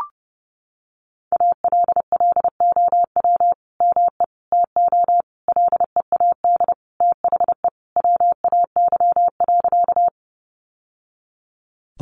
Morse code, heard as a keyed tone, about 30 wpm, 700 Hz. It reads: ALLOW ME TO LEAD THE WAY.
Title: Sentences from Top 700 Words 30wpm